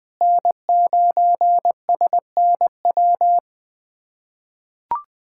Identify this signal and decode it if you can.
Morse code, heard as a keyed tone, about 20 wpm, 700 Hz. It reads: N9SNW
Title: SOTA QSO Practice 20wpm